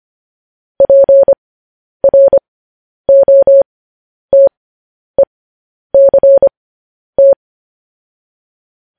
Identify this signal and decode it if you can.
Morse code, heard as a keyed tone, about 25 wpm, spaced as for 10 wpm, 550 Hz. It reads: PROTECT
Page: news-headlines-in-morse-code-at-10-wpm